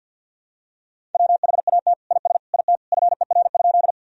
Morse code, heard as a keyed tone, about 46 wpm, 700 Hz. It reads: WHAT IS IT FER?
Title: Sentences from Top 100 Words and Abbreviations 50wpm